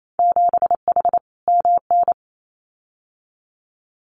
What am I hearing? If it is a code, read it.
Morse code, heard as a keyed tone, about 28 wpm, 700 Hz. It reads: 7H MN
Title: Winter Field Day Exchanges Encore 28wpm